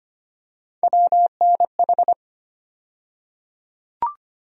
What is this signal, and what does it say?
Morse code, heard as a keyed tone, about 25 wpm, 700 Hz. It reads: WNH
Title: USA Call Sign Postfixes - 3 Letters 25wpm